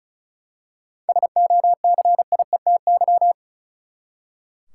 Morse code, heard as a keyed tone, about 35 wpm, 700 Hz. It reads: SOCIETY
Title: Seven Letter Words - 35wpm